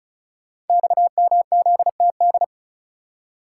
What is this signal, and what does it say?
Morse code, heard as a keyed tone, about 36 wpm, 700 Hz. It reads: XMZTD
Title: Code Groups - 5 Letter:Numbers - 35wpm